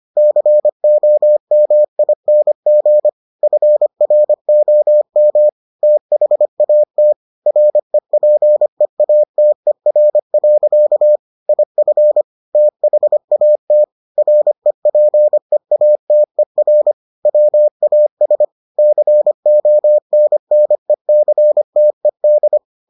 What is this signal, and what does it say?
Morse code, heard as a keyed tone, about 25 wpm, 600 Hz. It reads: COMING FROM THAT REPEATER. IF THAT REPEATER WAS CONNECTED